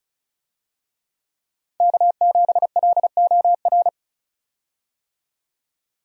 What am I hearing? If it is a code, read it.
Morse code, heard as a keyed tone, about 36 wpm, 700 Hz. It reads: K7LOR
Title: USA Call Signs - 5 Characters - 35wpm